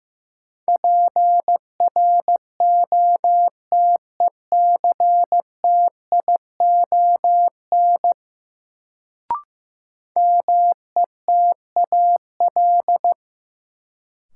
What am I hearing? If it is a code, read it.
Morse code, heard as a keyed tone, about 15 wpm, 700 Hz. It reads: PROTECTION METAL
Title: Sentences from Top 1600 Words - New Word Drill Alt - 15wpm